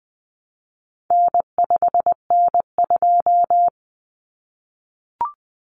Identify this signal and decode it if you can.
Morse code, heard as a keyed tone, about 20 wpm, 700 Hz. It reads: N5N2
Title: Common Cut Numbers - 4-Digit 20wpm